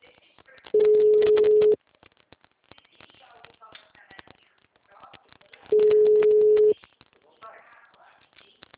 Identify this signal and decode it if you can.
Ringback tone (Europe, 425 Hz), with speech.